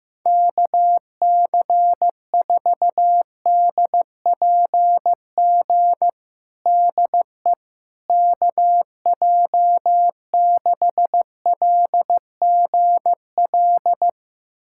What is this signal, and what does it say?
Morse code, heard as a keyed tone, about 15 wpm, 700 Hz. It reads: KC4DPG DE KJ6LGL
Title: USA Callsign DE Callsign - 6 Characters - 15wpm